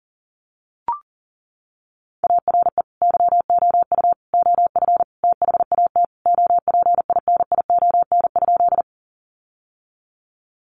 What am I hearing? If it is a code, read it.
Morse code, heard as a keyed tone, about 40 wpm, 700 Hz. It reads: ARE YOU OF THAT OPINION?
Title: Sentences from Top 1200 Words 40wpm